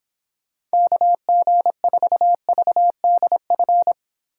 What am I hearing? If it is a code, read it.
Morse code, heard as a keyed tone, about 29 wpm, 700 Hz. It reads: KG4VDF